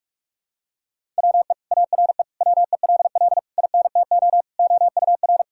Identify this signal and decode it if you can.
Morse code, heard as a keyed tone, about 41 wpm, 700 Hz. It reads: WE ARE WELL INTO OUR
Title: Sentences from Top 1000 Words - Review Part 5 - 45wpm